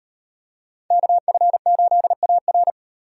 Morse code, heard as a keyed tone, about 38 wpm, 700 Hz. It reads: KF8AR